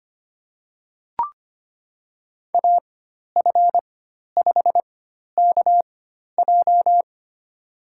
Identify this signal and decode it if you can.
Morse code, heard as a keyed tone, about 25 wpm, spaced as for 12 wpm, 700 Hz. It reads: AF5KJ
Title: USA Call Signs - 5 Characters - 4x Character Spacing - 25wpm